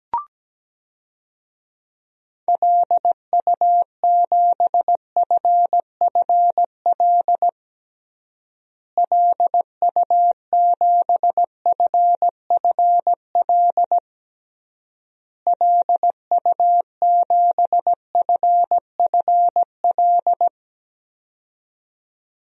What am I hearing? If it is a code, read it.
Morse code, heard as a keyed tone, about 17 wpm, 700 Hz. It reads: LU7FFL LU7FFL LU7FFL
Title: DXCC-100 Call Signs - 6 Characters - Repeated 3x - 17wpm